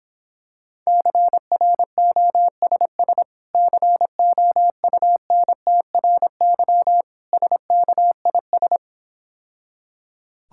Morse code, heard as a keyed tone, about 26 wpm, 700 Hz. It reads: CROSS COUNTRY SKIS